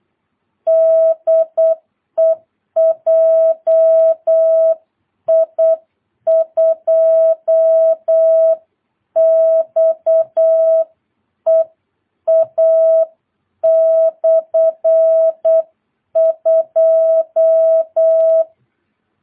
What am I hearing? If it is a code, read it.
Morse code, heard as a keyed tone, about 8 wpm, 660 Hz. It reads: DE JI2XEA/2
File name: jh8cbh / 2022cw